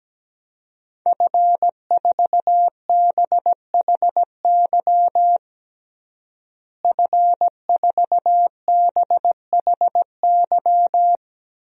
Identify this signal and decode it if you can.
Morse code, heard as a keyed tone, about 17 wpm, 700 Hz. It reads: F4BHY F4BHY
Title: DXCC-100 Call Signs - 5 Characters - Repeated 3x - 17wpm